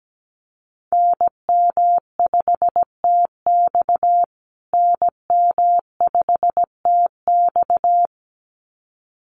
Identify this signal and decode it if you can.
Morse code, heard as a keyed tone, about 17 wpm, 700 Hz. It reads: NM5TX NM5TX
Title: USA Call Signs Repeated 2x 17wpm